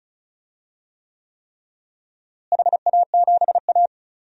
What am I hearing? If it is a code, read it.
Morse code, heard as a keyed tone, about 34 wpm, 700 Hz. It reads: HA7A
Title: DXCC-100 Call Signs - 4 Characters - Repeated 3x - 35wpm